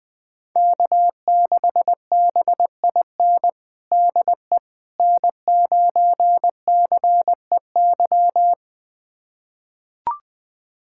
Morse code, heard as a keyed tone, about 20 wpm, 700 Hz. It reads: K6BIN DE N9CEY
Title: USA Callsign DE Callsign - 5 Characters - 20wpm